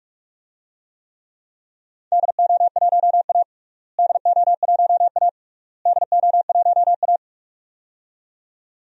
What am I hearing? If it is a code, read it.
Morse code, heard as a keyed tone, about 46 wpm, 700 Hz. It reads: DO1A DO1A DO1A
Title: DXCC-100 Call Signs - 4 Characters - Repeated 3x - 45wpm